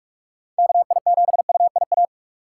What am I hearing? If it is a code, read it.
Morse code, heard as a keyed tone, about 43 wpm, 700 Hz. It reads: KI7UIA